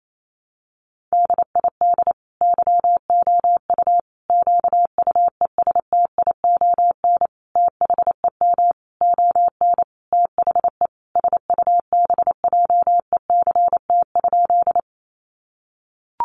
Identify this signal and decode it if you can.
Morse code, heard as a keyed tone, about 28 wpm, 700 Hz. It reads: DID YOU QUESTION THEM ON THE SUBJECT?
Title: Sentences from Top 800 Words 28wpm